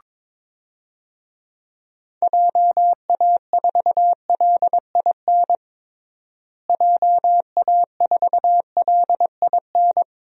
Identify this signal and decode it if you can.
Morse code, heard as a keyed tone, about 22 wpm, 700 Hz. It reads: JA4LIN JA4LIN
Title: DXCC-100 Call Signs - 6 Characters - Repeated 2x - 22wpm